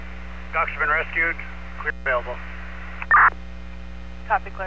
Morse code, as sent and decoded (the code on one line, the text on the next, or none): none